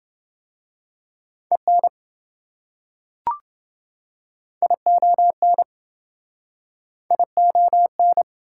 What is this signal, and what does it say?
Morse code, heard as a keyed tone, about 28 wpm, 700 Hz. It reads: EN ION ION